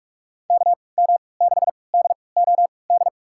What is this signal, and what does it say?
Morse code, heard as a keyed tone, about 45 wpm, spaced as for 25 wpm, 700 Hz. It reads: KM6DOD
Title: USA Call Signs - 6 Characters - 3x Character Spacing - 45wpm